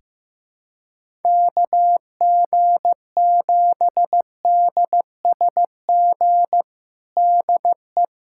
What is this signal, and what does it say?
Morse code, heard as a keyed tone, about 15 wpm, 700 Hz. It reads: KG7DSG DE
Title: USA Callsign DE Callsign - 6 Characters - 15wpm